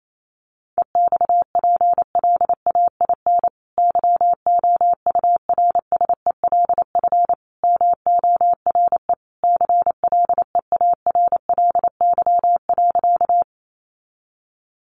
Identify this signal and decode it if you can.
Morse code, heard as a keyed tone, about 28 wpm, 700 Hz. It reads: EXPLAIN YOURSELF MORE CLEARLY.
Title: Sentences from Top 1300 Words 28wpm